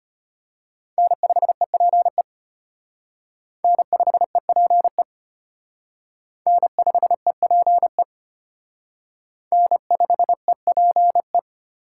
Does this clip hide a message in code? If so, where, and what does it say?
Yes: Morse code from 1.0 to 11.4 s, heard as a keyed tone, about 32 wpm, 700 Hz. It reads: N5EPE N5EPE N5EPE N5EPE